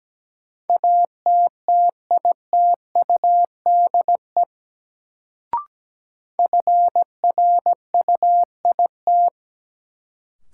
Morse code, heard as a keyed tone, about 17 wpm, 700 Hz. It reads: ATTITUDE FRUIT
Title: Sentences from Top 1300 Words - New Word Drill Alt - 17wpm